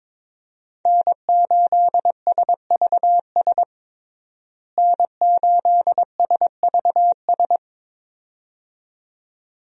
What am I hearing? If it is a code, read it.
Morse code, heard as a keyed tone, about 22 wpm, 700 Hz. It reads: N8SVS N8SVS
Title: USA Call Signs - 5 Characters - Repeated 2x - 22wpm